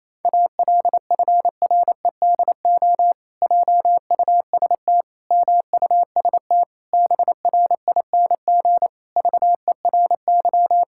Morse code, heard as a keyed tone, about 28 wpm, 700 Hz. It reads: ALFREDO JUST MUST BRING VERY
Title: Pangram Challenge v2 - 28wpm